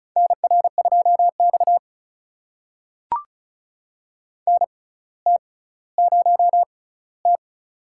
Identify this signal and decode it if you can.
Morse code, heard as a keyed tone, about 36 wpm, spaced as for 18 wpm, 700 Hz. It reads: NR2X NT0T
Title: USA Call Signs - 4 Characters - 6x Character Spacing - 35wpm